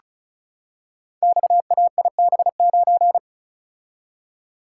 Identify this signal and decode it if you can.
Morse code, heard as a keyed tone, about 36 wpm, 700 Hz. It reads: XAIB9
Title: Code Groups - 5 Letter:Numbers - 35wpm